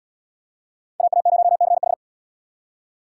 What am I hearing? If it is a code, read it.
Morse code, heard as a keyed tone, about 66 wpm, 700 Hz. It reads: AA0CH